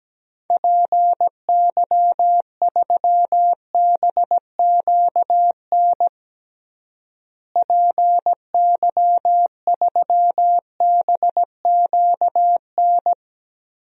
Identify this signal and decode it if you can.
Morse code, heard as a keyed tone, about 17 wpm, 700 Hz. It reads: PY3BQN PY3BQN
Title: DXCC-100 Call Signs - 6 Characters - Repeated 2x - 17wpm